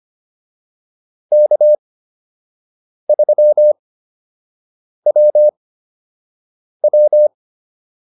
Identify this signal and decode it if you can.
Morse code, heard as a keyed tone, about 25 wpm, spaced as for 6 wpm, 600 Hz. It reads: K3WW